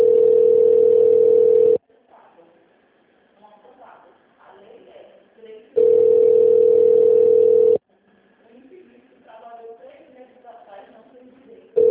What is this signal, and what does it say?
Ringback tone (North America), 440 + 480 Hz, with speech.
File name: out- -2001- - - .0